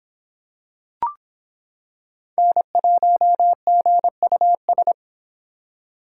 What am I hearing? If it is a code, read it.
Morse code, heard as a keyed tone, about 26 wpm, 700 Hz. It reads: N1GUS